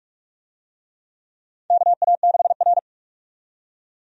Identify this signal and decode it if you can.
Morse code, heard as a keyed tone, about 45 wpm, 700 Hz. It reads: KA6R
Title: USA Call Signs - 4 Characters - 45wpm